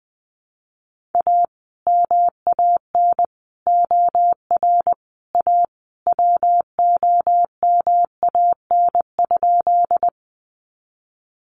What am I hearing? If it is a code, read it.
Morse code, heard as a keyed tone, about 20 wpm, 700 Hz. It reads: A MAN OR A WOMAN?
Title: Sentences from Top 1100 Words 20wpm